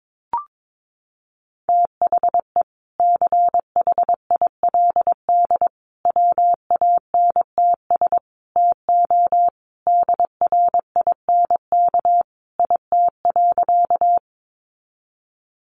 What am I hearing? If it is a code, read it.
Morse code, heard as a keyed tone, about 22 wpm, 700 Hz. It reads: THE CHILD WANTS TO DRINK IT.
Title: Additional Easy Phrases 22wpm